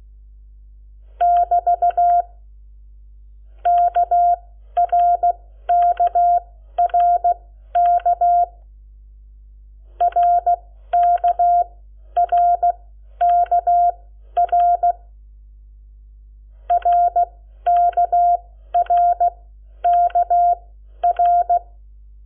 Morse code, heard as a keyed tone, about 16 wpm, spaced as for 11 wpm, 700 Hz. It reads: <BT> KRKRK RKRKR RKRKR
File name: Lecciones / CLASE Nº7